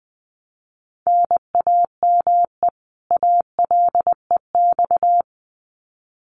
Morse code, heard as a keyed tone, about 20 wpm, 700 Hz. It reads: NAME ALEX
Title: Common QSO Phrases 20wpm